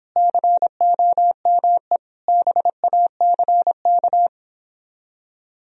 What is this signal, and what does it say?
Morse code, heard as a keyed tone, about 29 wpm, 700 Hz. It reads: COME BACK